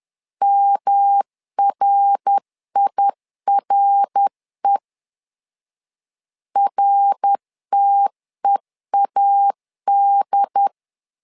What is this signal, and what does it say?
Morse code, heard as a keyed tone, about 10 wpm, 800 Hz. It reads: MRIRE RTEAD
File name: LEZIONI / 3 MWRO / 3 MWR 10wpm 1952